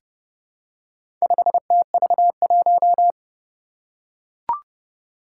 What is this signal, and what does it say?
Morse code, heard as a keyed tone, about 30 wpm, 700 Hz. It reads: 5TV1
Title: Four Challenging Letter-Number Combinations 30wpm